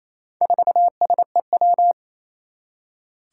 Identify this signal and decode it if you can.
Morse code, heard as a keyed tone, about 27 wpm, 700 Hz. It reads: 4SEW
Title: Four Challenging Letter-Number Combinations 28wpm